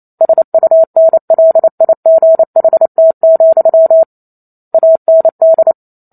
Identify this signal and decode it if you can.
Morse code, heard as a keyed tone, about 28 wpm, 650 Hz. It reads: SUNLIGHT, AND